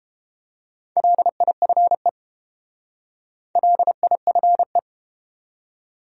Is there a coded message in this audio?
Yes: Morse code from 1.0 to 4.8 s, heard as a keyed tone, about 30 wpm, 700 Hz. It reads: LIFE LIFE